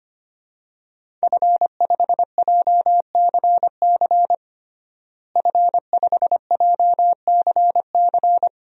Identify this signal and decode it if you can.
Morse code, heard as a keyed tone, about 25 wpm, 700 Hz. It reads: F5JCC F5JCC
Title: DXCC-100 Call Signs - 5 Characters - Repeated 3x - 25wpm